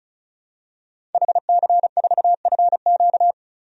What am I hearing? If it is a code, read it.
Morse code, heard as a keyed tone, about 36 wpm, 700 Hz. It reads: HC4FQ
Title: DXCC-100 Call Signs - 5 Characters - 35wpm